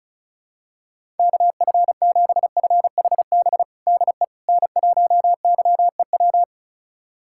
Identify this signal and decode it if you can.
Morse code, heard as a keyed tone, about 34 wpm, 690 Hz. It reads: KF7FHB DE N1YEW
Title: USA Callsign DE Callsign 35wpm